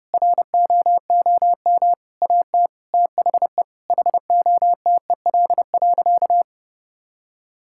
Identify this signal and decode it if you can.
Morse code, heard as a keyed tone, about 30 wpm, 700 Hz. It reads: ROOM AT THE HOTEL.
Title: Sentences from Top 1600 Words 30wpm